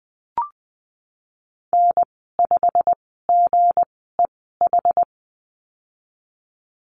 Morse code, heard as a keyed tone, about 20 wpm, spaced as for 15 wpm, 700 Hz. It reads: N5GEH
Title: 238 Lesson 38g - Letter q - Call Signs 20wpm